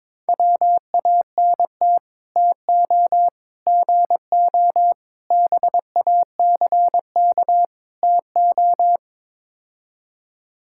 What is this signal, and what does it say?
Morse code, heard as a keyed tone, about 22 wpm, 700 Hz. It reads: WANT TO GO BACK TO
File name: practice / 22wpm / Build A Phrase - 5-Word Phrases - 22wpm